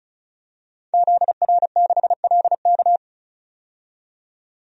Morse code, heard as a keyed tone, about 34 wpm, 700 Hz. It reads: ZR6LK
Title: DXCC-100 Call Signs - 5 Characters - 35wpm